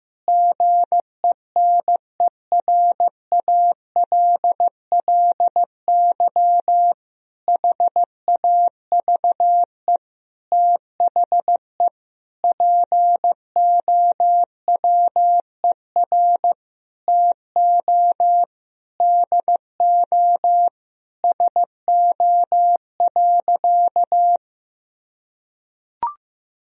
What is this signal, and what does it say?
Morse code, heard as a keyed tone, about 16 wpm, 700 Hz. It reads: GENERALLY HAVE THE POWER TO DO SO.
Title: Sentences from Top 1000 Words - Review Part 4 - 15wpm